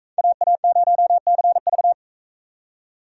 Morse code, heard as a keyed tone, about 44 wpm, 700 Hz. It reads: AA0CV